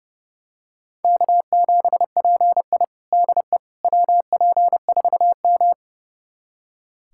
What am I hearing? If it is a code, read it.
Morse code, heard as a keyed tone, about 32 wpm, 700 Hz. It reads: K7PI DE WP4M
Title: USA Callsign DE Callsign - 4 Characters - 30wpm